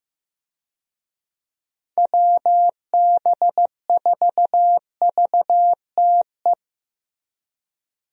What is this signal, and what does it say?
Morse code, heard as a keyed tone, about 15 wpm, 700 Hz. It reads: WB4VTE
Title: USA Call Signs - 6 Characters - Repeated 3x - 15wpm